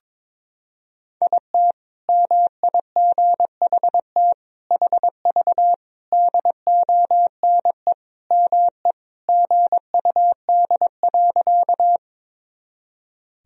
Morse code, heard as a keyed tone, about 22 wpm, 700 Hz. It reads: IT MIGHT HV DONE ME GUD.
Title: Sentences from Top 500 Words and Abbreviations 22wpm